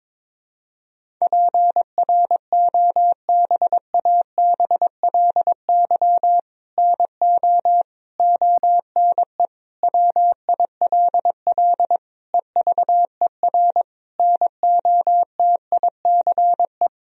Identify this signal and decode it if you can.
Morse code, heard as a keyed tone, about 22 wpm, 700 Hz. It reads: PROBABLY NO ONE WILL EVER NOTICE
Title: Sentences from Top 1000 Words - Review Part 4 - 22wpm